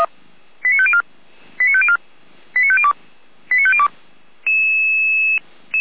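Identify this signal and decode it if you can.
Five-tone selective call (ZVEI), heard as quick runs of five tones, tones of about 70 ms. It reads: 89504 89504 89502 89502
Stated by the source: ZVEI five-tone call received by SDR, from a server of feuerwehr-orlamuende.org (fire brigade Orlamünde).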